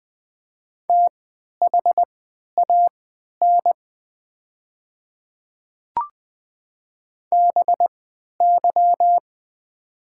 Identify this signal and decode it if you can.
Morse code, heard as a keyed tone, about 20 wpm, spaced as for 11 wpm, 700 Hz. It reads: THAN BY
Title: 3x Character Spacing - Top 200 Words - 20wpm